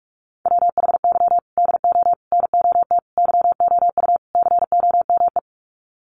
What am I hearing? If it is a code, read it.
Morse code, heard as a keyed tone, about 44 wpm, 700 Hz. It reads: WHY DO NOT YOU COME
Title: Sets of 5 Words Encore2 45wpm